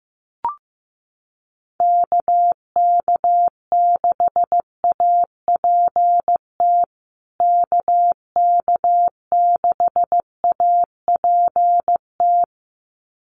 Morse code, heard as a keyed tone, about 15 wpm, 700 Hz. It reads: KK6APT KK6APT
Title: USA Call Signs Repeated 2x 15wpm